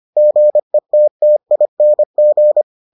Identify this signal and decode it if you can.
Morse code, heard as a keyed tone, about 25 wpm, 600 Hz. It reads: GETTING